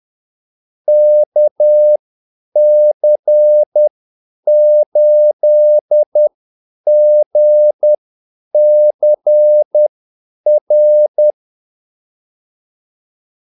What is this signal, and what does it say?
Morse code, heard as a keyed tone, about 10 wpm, spaced as for 8 wpm, 600 Hz. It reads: KC8GCR
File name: Callsigns-10-8wpm